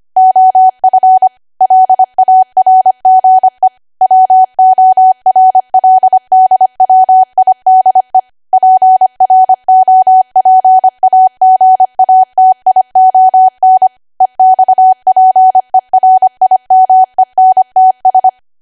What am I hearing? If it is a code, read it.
Morse code, heard as a keyed tone, about 25 wpm, 750 Hz. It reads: OF LARGE WORLDWIDE PROPAGATION EXPERIMENTS
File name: cwa / W1AW-Level3 / 25WPM